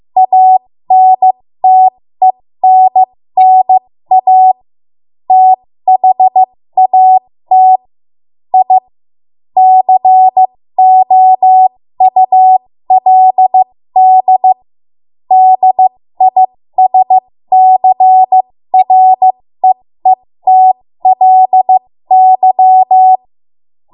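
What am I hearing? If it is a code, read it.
Morse code, heard as a keyed tone, about 15 wpm, spaced as for 13 wpm, 750 Hz. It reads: ANTENNA THAT I COULD DISCREETLY
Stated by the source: 13 wpm Morse code practice archive transmission.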